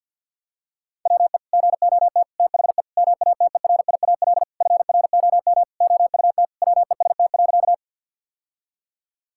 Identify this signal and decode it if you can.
Morse code, heard as a keyed tone, about 52 wpm, 700 Hz. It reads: WE GOT THE MATERIAL FROM OUT WEST.
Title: Sentences from Top 1000 Words - Review Part 5 - 50wpm